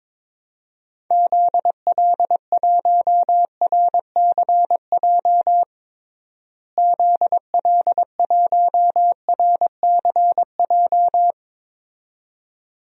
Morse code, heard as a keyed tone, about 21 wpm, 700 Hz. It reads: ZL1RCJ ZL1RCJ
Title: DXCC-100 Call Signs - 6 Characters - Repeated 2x - 22wpm